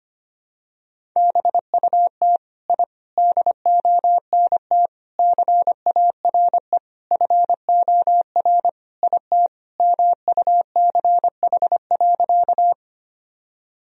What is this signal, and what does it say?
Morse code, heard as a keyed tone, about 25 wpm, 700 Hz. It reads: BUT I DONT CARE FOR IT MUCH.
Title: Sentences from Top 1000 Words - Review Part 1 - 25wpm